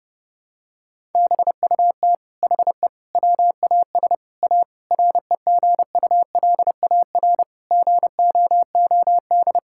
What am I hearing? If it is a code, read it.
Morse code, heard as a keyed tone, about 28 wpm, 700 Hz. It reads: BUT HE WAS A REGULAR GOOD
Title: Sentences from Top 1000 Words - Review Part 6 - 30wpm